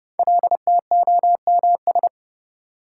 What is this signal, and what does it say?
Morse code, heard as a keyed tone, about 30 wpm, 700 Hz. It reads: LTOMS
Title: Code Groups - 5 Letter:Numbers - 30wpm